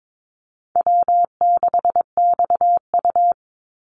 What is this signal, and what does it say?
Morse code, heard as a keyed tone, about 22 wpm, 700 Hz. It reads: W6XU
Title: Easy Call Signs - 22wpm